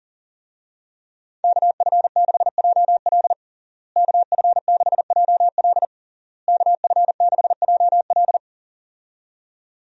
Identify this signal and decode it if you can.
Morse code, heard as a keyed tone, about 40 wpm, 700 Hz. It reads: KF6JL KF6JL KF6JL
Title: USA Call Signs - 5 Characters - Repeated 3x - 40wpm